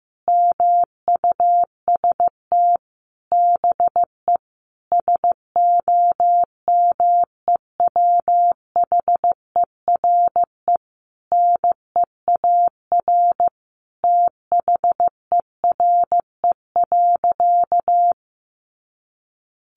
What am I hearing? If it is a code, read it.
Morse code, heard as a keyed tone, about 15 wpm, 700 Hz. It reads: MUST BE SOMEWHERE NEAR THERE.
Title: Sentences from Top 1400 Words 15wpm